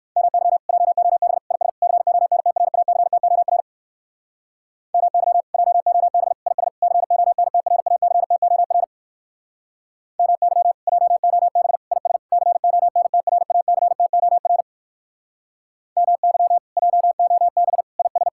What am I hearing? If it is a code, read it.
Morse code, heard as a keyed tone, about 56 wpm, 700 Hz. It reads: MY JOB IS CONTRACTOR MY JOB IS CONTRACTOR MY JOB IS CONTRACTOR MY JOB IS